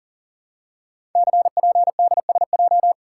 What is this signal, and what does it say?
Morse code, heard as a keyed tone, about 38 wpm, 700 Hz. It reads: CPDSJ
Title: Code Groups - 5 Letter:Numbers - 40wpm